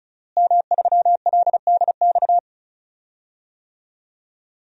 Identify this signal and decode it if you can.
Morse code, heard as a keyed tone, about 34 wpm, 700 Hz. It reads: M3LDX
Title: DXCC-100 Call Signs - 5 Characters - Repeated 2x - 35wpm